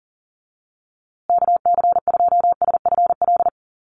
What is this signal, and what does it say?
Morse code, heard as a keyed tone, about 40 wpm, 700 Hz. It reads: KC2SFL
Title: USA Call Signs Repeated 2x 40wpm